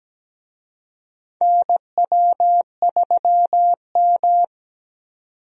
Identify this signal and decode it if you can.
Morse code, heard as a keyed tone, about 17 wpm, 700 Hz. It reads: NW3M
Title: USA Call Signs - 4 Characters - 17wpm